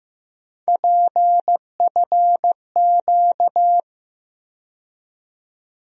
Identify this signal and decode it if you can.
Morse code, heard as a keyed tone, about 15 wpm, 700 Hz. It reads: PFQ
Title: USA Call Sign Postfixes - 3 Letters 15wpm